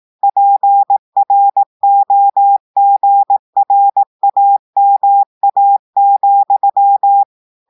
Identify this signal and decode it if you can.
Morse code, heard as a keyed tone, about 18 wpm, 800 Hz. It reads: PROGRAMA,
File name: audios / portugais / portugais texte en clair 2 vitesse 18 mots minutes